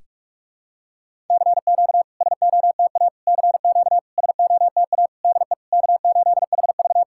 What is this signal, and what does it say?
Morse code, heard as a keyed tone, about 46 wpm, 690 Hz. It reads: CQ SOTA CQ SOTA DE K8HV
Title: SOTA QSO Practice 45wpm